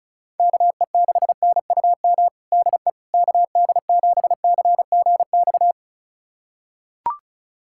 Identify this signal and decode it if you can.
Morse code, heard as a keyed tone, about 36 wpm, 700 Hz. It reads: KE6NUM DE KD7CGX
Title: USA Callsign DE Callsign - 6 Characters - 35wpm